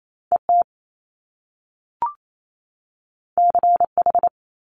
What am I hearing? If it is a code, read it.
Morse code, heard as a keyed tone, about 28 wpm, 700 Hz. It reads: E T C H
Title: Top Two Letter Combinations in Words 28wpm